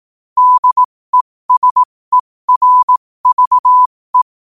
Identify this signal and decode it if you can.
Morse code, heard as a keyed tone, about 18 wpm, spaced as for 15 wpm, 1000 Hz. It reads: DESERVE